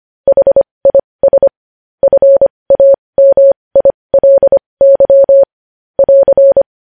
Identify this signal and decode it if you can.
Morse code, heard as a keyed tone, about 25 wpm, spaced as for 20 wpm, 550 Hz. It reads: HIS FAMILY <AR>
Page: news-headlines-in-morse-code-at-20-wpm